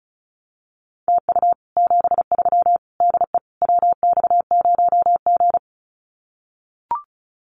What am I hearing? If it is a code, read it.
Morse code, heard as a keyed tone, about 34 wpm, 700 Hz. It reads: TU 73 DE WX0G
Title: Common QSO Phrases Encore 35wpm